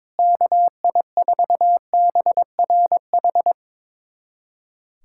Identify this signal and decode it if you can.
Morse code, heard as a keyed tone, about 22 wpm, 700 Hz. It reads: KI4BRH
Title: USA Callsign DE Callsign 22wpm